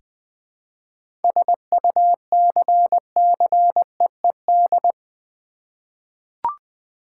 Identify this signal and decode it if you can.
Morse code, heard as a keyed tone, about 20 wpm, 700 Hz. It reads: SUCCEED
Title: Seven Letter Words - 20wpm